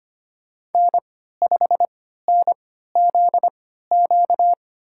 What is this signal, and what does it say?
Morse code, heard as a keyed tone, about 25 wpm, spaced as for 14 wpm, 700 Hz. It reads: N5NZQ
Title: USA Call Signs - 3x Character Spacing - 25wpm